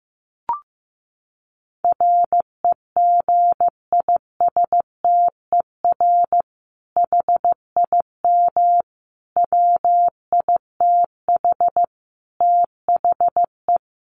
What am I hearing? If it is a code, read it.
Morse code, heard as a keyed tone, about 15 wpm, 700 Hz. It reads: REGISTER HIM WITH THE
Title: Sentences from Top 1200 Words 15wpm